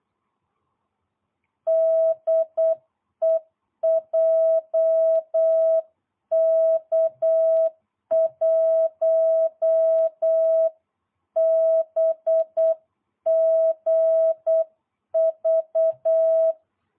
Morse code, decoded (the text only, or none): DE JK1BGV